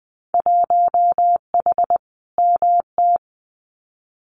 Morse code, heard as a keyed tone, about 20 wpm, 700 Hz. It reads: 1H MT
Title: Winter Field Day Exchanges Encore 20wpm